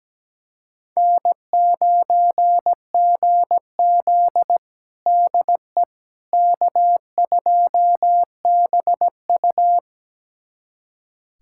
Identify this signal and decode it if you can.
Morse code, heard as a keyed tone, about 17 wpm, 700 Hz. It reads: N9GZ DE K2BU
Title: USA Callsign DE Callsign - 4 Characters - 17wpm